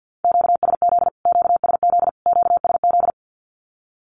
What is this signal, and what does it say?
Morse code, heard as a keyed tone, about 51 wpm, 700 Hz. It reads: QSZ QSZ QSZ
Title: Q-Codes Repeated 3x 50wpm